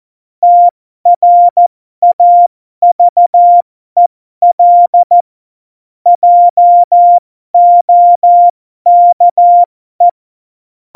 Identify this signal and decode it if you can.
Morse code, heard as a keyed tone, about 14 wpm, spaced as for 12 wpm, 700 Hz. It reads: TRAVEL JOKE
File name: Morsekurs / Witze